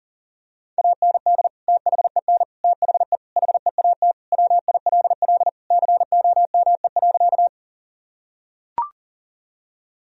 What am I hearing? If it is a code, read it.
Morse code, heard as a keyed tone, about 40 wpm, 700 Hz. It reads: AND THEN THE HEAT WILL COME.
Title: Sentences from Top 1000 Words - Review Part 3 - 40wpm